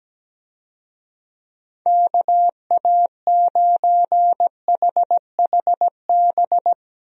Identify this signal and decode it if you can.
Morse code, heard as a keyed tone, about 17 wpm, 700 Hz. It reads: KA9HHB
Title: USA Call Signs - 6 Characters - Repeated 3x - 17wpm